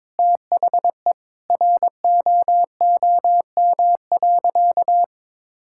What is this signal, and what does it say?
Morse code, heard as a keyed tone, about 21 wpm, 700 Hz. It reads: THE ROOM.
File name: practice / 22wpm / Sentences with 3-Word Phrases - 22wpm